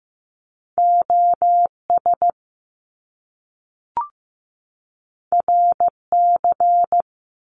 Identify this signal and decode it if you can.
Morse code, heard as a keyed tone, about 15 wpm, 700 Hz. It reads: OS RC
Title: Top Two Letter Combinations in Words 15wpm